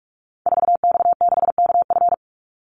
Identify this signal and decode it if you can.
Morse code, heard as a keyed tone, about 44 wpm, 700 Hz. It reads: 4X6KF
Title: Call Signs Repeated 3x - 45wpm